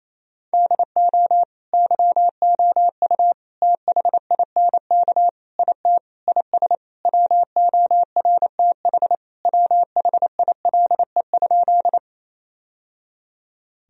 Morse code, heard as a keyed tone, about 28 wpm, 700 Hz. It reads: DO YOU THINK IT IS WORTH WHILE?
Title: Sentences from Top 1000 Words - Review Part 1 - 28wpm